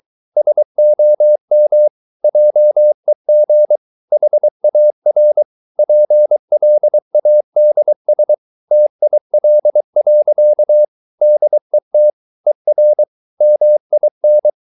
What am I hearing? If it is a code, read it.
Morse code, heard as a keyed tone, about 23 wpm, 600 Hz. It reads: SOM JEG HAR PLADS TIL. DET ER MIN
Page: Ferie 23 wpm